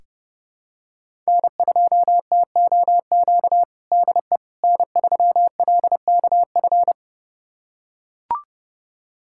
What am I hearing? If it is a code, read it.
Morse code, heard as a keyed tone, about 30 wpm, 700 Hz. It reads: N2TOQ DE N3LKF
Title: USA Callsign DE Callsign - 5 Characters - 30wpm